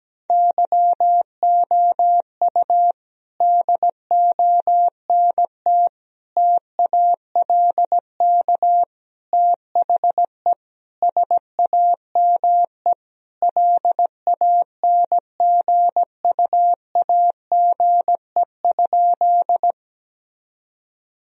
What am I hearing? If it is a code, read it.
Morse code, heard as a keyed tone, about 17 wpm, 700 Hz. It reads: YOU DONT TALK THE SAME LANGUAGE?
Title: Sentences from Top 1000 Words - Review Part 3 - 17wpm